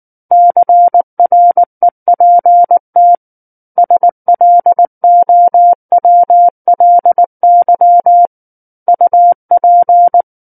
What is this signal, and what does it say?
Morse code, heard as a keyed tone, about 19 wpm, 700 Hz. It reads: CREPT SLOWLY UP
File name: Book One / 10-Chapter 10 - 19 WPM